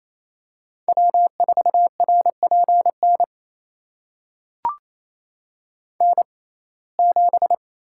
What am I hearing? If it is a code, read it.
Morse code, heard as a keyed tone, about 27 wpm, spaced as for 20 wpm, 700 Hz. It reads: W4RPN N7
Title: USA Call Signs - 5 Characters - 6x Character Spacing - 28wpm